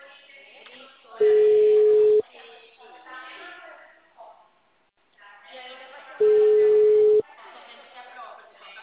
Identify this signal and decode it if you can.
Ringback tone (Europe, 425 Hz), with speech.
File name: out- -2044- - - .4175